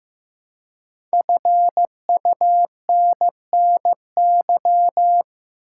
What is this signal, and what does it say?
Morse code, heard as a keyed tone, about 15 wpm, 700 Hz. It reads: FUNNY
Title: Five Letter Words - 15wpm